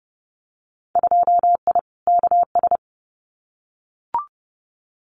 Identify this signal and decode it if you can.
Morse code, heard as a keyed tone, about 30 wpm, 700 Hz. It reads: 2I KS
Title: Winter Field Day Exchanges Encore 30wpm